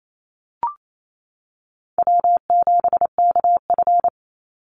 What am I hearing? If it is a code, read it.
Morse code, heard as a keyed tone, about 28 wpm, 700 Hz. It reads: W7KF
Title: Call Signs - 28wpm